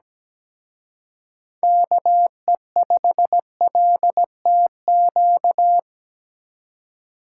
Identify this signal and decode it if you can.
Morse code, heard as a keyed tone, about 20 wpm, 700 Hz. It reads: KE5LTQ
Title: Speed Racing - USA 2x3 Call Signs 17wpm